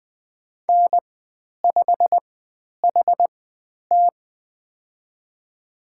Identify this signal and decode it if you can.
Morse code, heard as a keyed tone, about 20 wpm, spaced as for 10 wpm, 700 Hz. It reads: N5HT
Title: 095 Lesson 16g - Letter c - Call Signs 20wpm